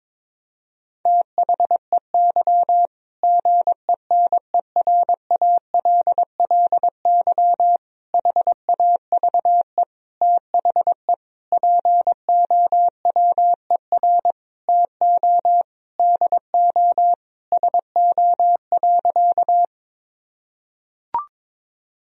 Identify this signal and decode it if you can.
Morse code, heard as a keyed tone, about 22 wpm, 700 Hz. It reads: THEY GENERALLY HAVE THE POWER TO DO SO.
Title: Sentences from Top 1000 Words - Review Part 4 - 22wpm